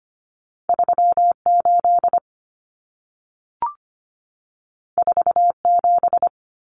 Morse code, heard as a keyed tone, about 25 wpm, 700 Hz. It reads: 38 47
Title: Numbers - 2-Digit 25wpm